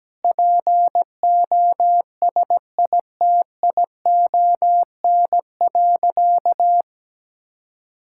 Morse code, heard as a keyed tone, about 17 wpm, 700 Hz. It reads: POSITION.
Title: Sentences from Top 1000 Words - Review Part 5 - 17wpm